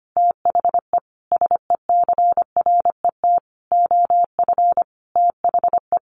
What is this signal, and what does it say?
Morse code, heard as a keyed tone, about 25 wpm, 700 Hz. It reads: THE SECRET OF THE
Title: Sentences from Top 1300 Words 25wpm